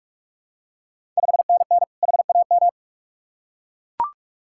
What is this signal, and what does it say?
Morse code, heard as a keyed tone, about 44 wpm, 690 Hz. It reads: 5NN HAM
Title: Ontario QSO Party Exchange 45wpm